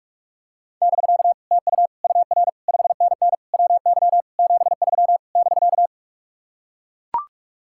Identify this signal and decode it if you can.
Morse code, heard as a keyed tone, about 45 wpm, 700 Hz. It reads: <BK> TU UR 5NN WY 73 <BK>